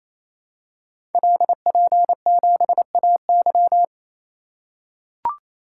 Morse code, heard as a keyed tone, about 27 wpm, 700 Hz. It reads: LP7AY
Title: Code Groups - 5 Letter:Numbers - 28wpm